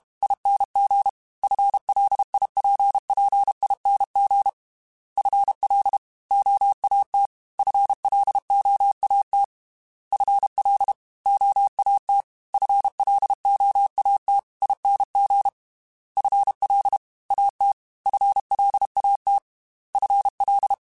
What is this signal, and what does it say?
Morse code, heard as a keyed tone, about 32 wpm, 800 Hz. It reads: ING FLIPPING FL OAT FLOAT FL OAT FLOAT ING FL AT FLAT FL
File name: FL / FL mp3s / Fast FL @30